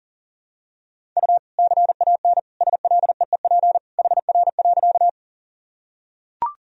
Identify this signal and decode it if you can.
Morse code, heard as a keyed tone, about 37 wpm, 700 Hz. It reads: U CAN SLEEP HR.
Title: Sentences from Top 500 Words and Abbreviations 40wpm